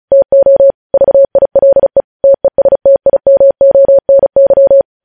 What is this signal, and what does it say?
Morse code, heard as a keyed tone, about 36 wpm, 550 Hz. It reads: TO VILE TESTIMONY
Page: News Headlines in Morse Code at 35 WPM